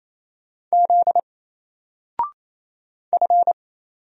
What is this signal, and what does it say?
Morse code, heard as a keyed tone, about 27 wpm, 700 Hz. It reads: Z F
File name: Single-Letter-Number-28wpm